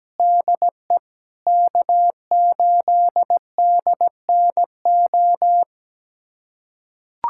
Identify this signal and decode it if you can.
Morse code, heard as a keyed tone, about 17 wpm, 700 Hz. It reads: DE K8DNO
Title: USA Callsign DE Callsign 17wpm